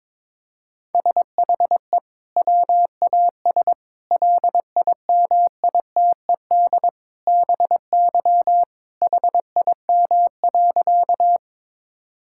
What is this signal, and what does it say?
Morse code, heard as a keyed tone, about 22 wpm, 700 Hz. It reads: SHE WAS LIMITED BY HIM.
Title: Sentences from Top 1000 Words - Review Part 6 - 22wpm